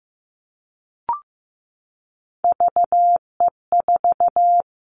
Morse code, heard as a keyed tone, about 15 wpm, 700 Hz. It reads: VE4
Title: Callsign Prefixes for Top 100 DXCC - 15wpm